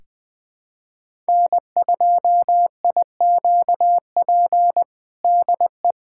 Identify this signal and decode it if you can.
Morse code, heard as a keyed tone, about 21 wpm, 700 Hz. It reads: N2IQP DE
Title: USA Callsign DE Callsign - 5 Characters - 20wpm